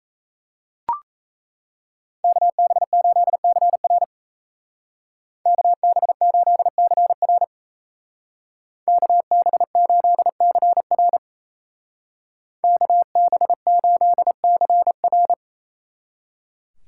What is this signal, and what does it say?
Morse code, heard as a keyed tone, about 37 wpm, 700 Hz. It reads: KB8CR KB8CR KB8CR KB8CR